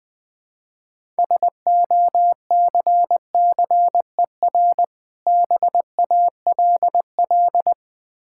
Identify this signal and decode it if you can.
Morse code, heard as a keyed tone, about 20 wpm, 700 Hz. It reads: SOCCER BALL
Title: Difficult Words with Hints - 20wpm